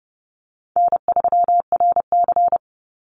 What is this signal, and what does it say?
Morse code, heard as a keyed tone, about 30 wpm, 700 Hz. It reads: N3RC
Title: Easy Call Signs - 30wpm